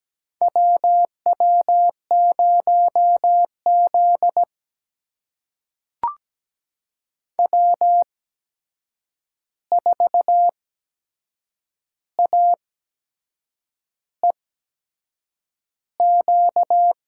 Morse code, heard as a keyed tone, about 17 wpm, spaced as for 6 wpm, 700 Hz. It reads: W W0Z W4AEQ
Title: USA Call Signs - 8x Character Spacing - 17wpm